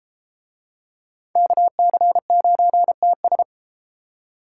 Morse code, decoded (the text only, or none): KC9TS